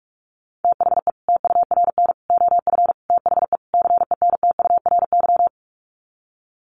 Morse code, heard as a keyed tone, about 44 wpm, 700 Hz. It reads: THE TURN OF THE CENTURY
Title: Sets of 5 Words 45wpm